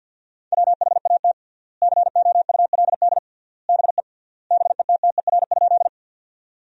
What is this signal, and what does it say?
Morse code, heard as a keyed tone, about 50 wpm, 700 Hz. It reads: WHAT COULD BE BETTER?
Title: 3x Word-Spacing - Sentences from Top 400 Words - 50wpm